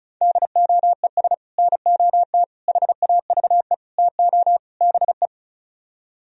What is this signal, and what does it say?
Morse code, heard as a keyed tone, about 45 wpm, 700 Hz. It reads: DOES NOT HAVE TO BE